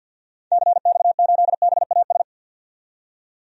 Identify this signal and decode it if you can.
Morse code, heard as a keyed tone, about 45 wpm, 700 Hz. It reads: CX8BAS
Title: DXCC-100 Call Signs - 6 Characters - 50wpm